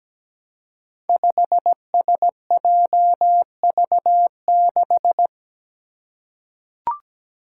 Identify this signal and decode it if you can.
Morse code, heard as a keyed tone, about 17 wpm, 700 Hz. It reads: 5SJV6
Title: Five Challenging Letter-Number Combinations 17wpm